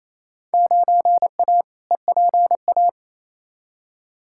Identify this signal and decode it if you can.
Morse code, heard as a keyed tone, about 28 wpm, 700 Hz. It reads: 9A EPA
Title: Field Day Exchanges Encore - 28wpm